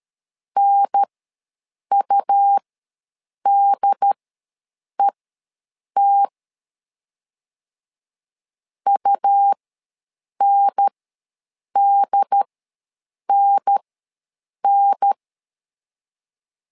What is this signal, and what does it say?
Morse code, heard as a keyed tone, about 13 wpm, spaced as for 7 wpm, 800 Hz. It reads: NUDET UNDNN
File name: LEZIONI / 2 UHDN / 2 UHDN ws 12wpm 1708